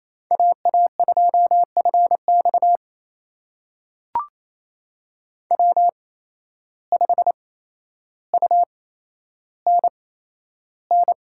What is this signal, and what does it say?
Morse code, heard as a keyed tone, about 27 wpm, spaced as for 12 wpm, 700 Hz. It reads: AA2FX W5UNN
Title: USA Call Signs - 5 Characters - 8x Character Spacing - 28wpm